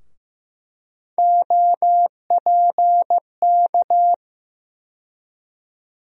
Morse code, heard as a keyed tone, about 15 wpm, 700 Hz. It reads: OPK
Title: USA Call Sign Postfixes - 3 Letters 15wpm